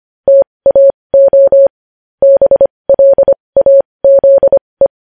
Morse code, heard as a keyed tone, about 25 wpm, spaced as for 20 wpm, 550 Hz. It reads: TAO BLAZE
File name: podcast / News headlines-1x-25x20